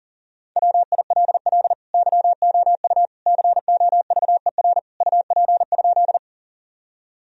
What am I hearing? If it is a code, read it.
Morse code, heard as a keyed tone, about 37 wpm, 700 Hz. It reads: WILL YOU COVER UP?
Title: Sentences from Top 1000 Words - Review Part 1 - 40wpm